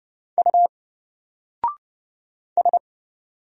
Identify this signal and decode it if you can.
Morse code, heard as a keyed tone, about 30 wpm, 700 Hz. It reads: U S
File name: Single-Letter-Number-30wpm